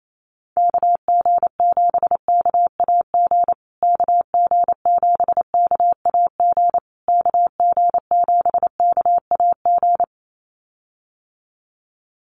Morse code, heard as a keyed tone, about 28 wpm, 690 Hz. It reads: KG7KAG KG7KAG KG7KAG
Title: USA Call Signs Repeated 3x 28wpm